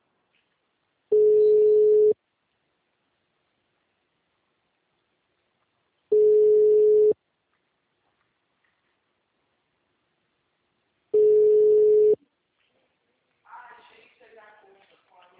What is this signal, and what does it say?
Ringback tone (Europe, 425 Hz), with speech.
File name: out- -2002- - - .8154